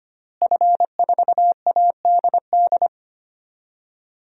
Morse code, heard as a keyed tone, about 25 wpm, 700 Hz. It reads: F4ADD
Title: DXCC-100 Call Signs - 5 Characters - Repeated 2x - 25wpm